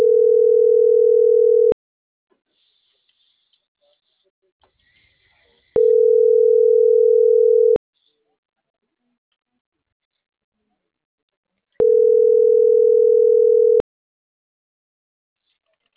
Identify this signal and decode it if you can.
Ringback tone (North America), 440 + 480 Hz.